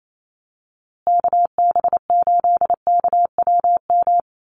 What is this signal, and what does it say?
Morse code, heard as a keyed tone, about 28 wpm, 700 Hz. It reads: KB8KWM
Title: USA Call Signs Repeated 3x 28wpm